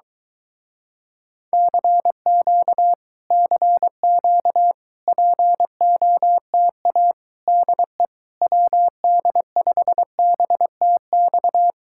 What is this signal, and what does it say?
Morse code, heard as a keyed tone, about 22 wpm, 700 Hz. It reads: CQ CQ POTA DE WD5BTX